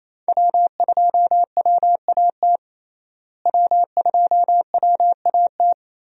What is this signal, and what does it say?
Morse code, heard as a keyed tone, about 27 wpm, 700 Hz. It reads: W2WAT W2WAT
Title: USA Call Signs - 5 Characters - Repeated 3x - 28wpm